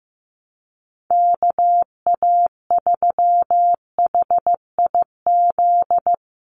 Morse code, heard as a keyed tone, about 15 wpm, 700 Hz. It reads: KA3HIZ
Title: USA Call Signs 15wpm